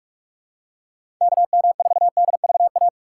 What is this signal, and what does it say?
Morse code, heard as a keyed tone, about 46 wpm, 700 Hz. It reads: KM4DVA
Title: USA Call Signs - 6 Characters - 3x Character Spacing - 45wpm